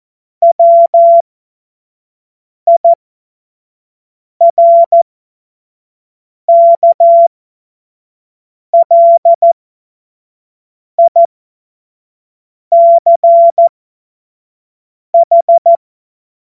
Morse code, heard as a keyed tone, about 14 wpm, spaced as for 5 wpm, 680 Hz. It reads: WIRKLICH
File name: Morsekurs / Witze / Witz 09. Februar 2021, 25 F70